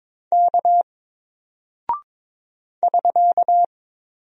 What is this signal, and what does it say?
Morse code, heard as a keyed tone, about 22 wpm, 700 Hz. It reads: K <SK>
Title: Single Character (Letter:Number:Punctuation:Prosign) Alt 22wpm